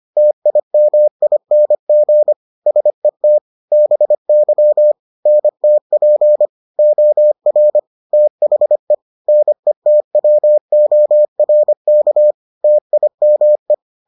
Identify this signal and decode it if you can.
Morse code, heard as a keyed tone, about 25 wpm, 600 Hz. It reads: TIMING SET BY NTP OR THE NETWORK TIME